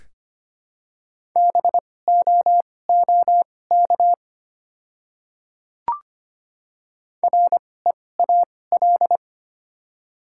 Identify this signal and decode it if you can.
Morse code, heard as a keyed tone, about 25 wpm, spaced as for 18 wpm, 690 Hz. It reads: BOOK REAL
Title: 2x Character Spacing - Top 300 Words - 25wpm